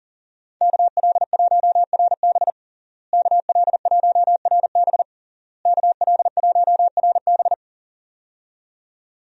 Morse code, heard as a keyed tone, about 37 wpm, 700 Hz. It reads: KL1RB KL1RB KL1RB
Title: USA Call Signs - 5 Characters - Repeated 3x - 40wpm